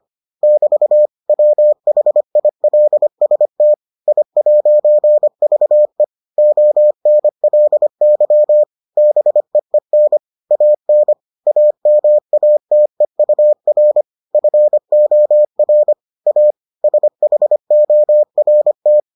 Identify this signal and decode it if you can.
Morse code, heard as a keyed tone, about 25 wpm, 600 Hz. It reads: <BT> WHILST I'VE ONLY BEEN AN AMATEUR FOR A SHORT